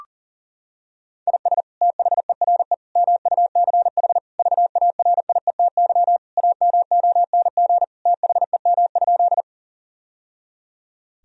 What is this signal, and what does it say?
Morse code, heard as a keyed tone, about 40 wpm, 700 Hz. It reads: IS THERE MUCH VARIETY AMONG THEM?
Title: Sentences from Top 1000 Words - Review Part 4 - 40wpm